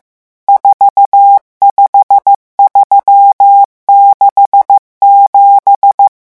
Morse code, heard as a keyed tone, about 15 wpm, 800 Hz. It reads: 45367
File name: morse mp3 / revision11 chiffres vitesse 14